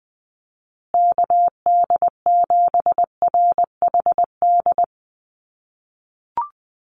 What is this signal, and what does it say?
Morse code, heard as a keyed tone, about 20 wpm, 700 Hz. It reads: KD7RHD
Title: USA Call Signs Repeated 3x 20wpm